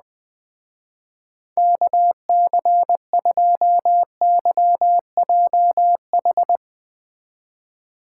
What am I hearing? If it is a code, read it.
Morse code, heard as a keyed tone, about 20 wpm, 700 Hz. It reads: KC2YJH